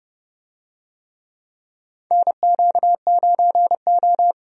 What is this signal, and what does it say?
Morse code, heard as a keyed tone, about 32 wpm, 700 Hz. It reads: NQ9O
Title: USA Call Signs - 4 Characters - 30wpm